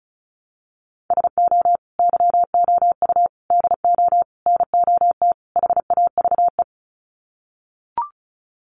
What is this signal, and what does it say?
Morse code, heard as a keyed tone, about 34 wpm, 700 Hz. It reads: SO YOU DO NOT HAVE
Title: Sets of 5 Words Encore 35wpm